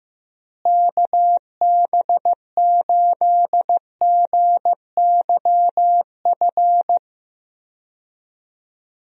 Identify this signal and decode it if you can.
Morse code, heard as a keyed tone, about 15 wpm, 700 Hz. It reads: KB8GYF
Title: USA Call Signs - 6 Characters - Repeated 3x - 15wpm